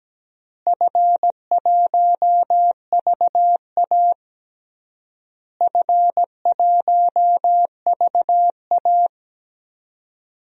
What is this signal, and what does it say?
Morse code, heard as a keyed tone, about 17 wpm, 700 Hz. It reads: F1VA F1VA
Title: DXCC-100 Call Signs - 4 Characters - Repeated 3x - 17wpm